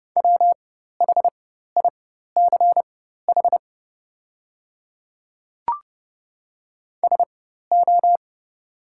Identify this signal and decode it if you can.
Morse code, heard as a keyed tone, about 30 wpm, spaced as for 14 wpm, 700 Hz. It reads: WHICH SO
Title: 4x Character Spacing - Top 100 Words - 30wpm